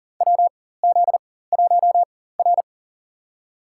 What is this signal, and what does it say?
Morse code, heard as a keyed tone, about 40 wpm, spaced as for 19 wpm, 700 Hz. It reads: WZ1R
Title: USA Call Signs - 4 Characters - 4x Character Spacing - 40wpm